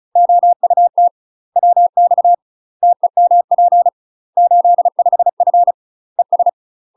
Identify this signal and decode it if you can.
Morse code, heard as a keyed tone, about 36 wpm, 700 Hz. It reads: OUT WX TEMP 85F ES
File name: UK MASTER QSOs TRAINING / 35WPM / UK MASTER QSOs 0016